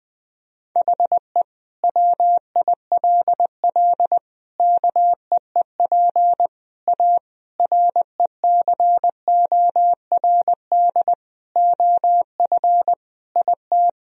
Morse code, heard as a keyed tone, about 20 wpm, 700 Hz. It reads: HE WILL KEEP A RECORD OF IT
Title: Sentences from Top 1000 Words - Review Part 3 - 20wpm